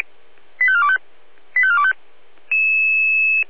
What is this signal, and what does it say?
Five-tone selective call (ZVEI), heard as quick runs of five tones, tones of about 70 ms. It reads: 85425 85425